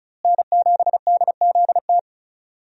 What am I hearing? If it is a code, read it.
Morse code, heard as a keyed tone, about 34 wpm, 700 Hz. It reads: N7DZT
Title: USA Call Signs - 5 Characters - Repeated 2x - 35wpm